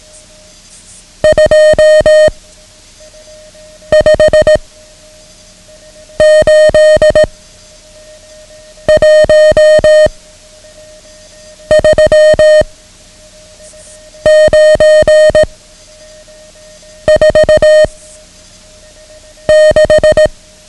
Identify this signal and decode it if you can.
Morse code, heard as a keyed tone, about 18 wpm, 620 Hz. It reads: 2 5 8 1 3 9 4 6
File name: ham radio / CodeQuick / Code Quick 09